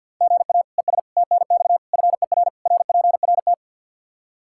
Morse code, heard as a keyed tone, about 49 wpm, 700 Hz. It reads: GA ES TNX FER RPRT
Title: Practice QSO - Exchange 1 - Beginning - 50wpm